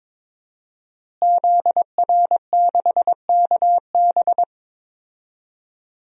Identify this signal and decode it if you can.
Morse code, heard as a keyed tone, about 22 wpm, 700 Hz. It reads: ZR6KB
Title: DXCC-100 Call Signs - 5 Characters - Repeated 2x - 22wpm